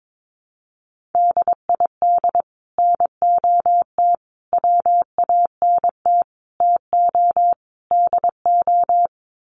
Morse code, heard as a keyed tone, about 22 wpm, 700 Hz. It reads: DID NOT WANT TO DO
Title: Sets of 5 Words Encore 22wpm